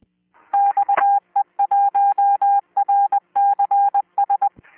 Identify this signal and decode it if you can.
Morse code, heard as a keyed tone, about 20 wpm, 800 Hz. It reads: XE1RCS